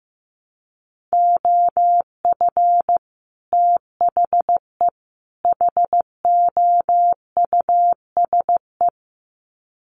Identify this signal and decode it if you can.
Morse code, heard as a keyed tone, about 15 wpm, 700 Hz. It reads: OF THE HOUSE
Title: Sets of 3 Words 15wpm